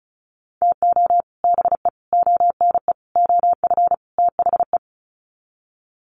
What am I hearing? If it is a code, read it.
Morse code, heard as a keyed tone, about 34 wpm, 700 Hz. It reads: TO BE ONE OF THE
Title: Sets of 5 Words 35wpm